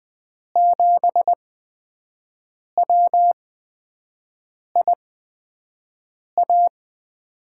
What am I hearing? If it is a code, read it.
Morse code, heard as a keyed tone, about 20 wpm, spaced as for 5 wpm, 700 Hz. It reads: 7WIA
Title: USA Call Signs - 5 Characters - 8x Character Spacing - 20wpm